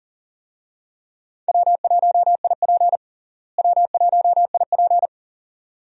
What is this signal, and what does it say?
Morse code, heard as a keyed tone, about 40 wpm, 700 Hz. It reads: W1IP W1IP
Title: USA Call Signs - 4 Characters - Repeated 2x - 40wpm